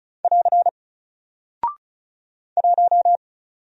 Morse code, heard as a keyed tone, about 36 wpm, 700 Hz. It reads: <AR> 1
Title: Single Character (Letter:Number:Punctuation:Prosign) Alt 35wpm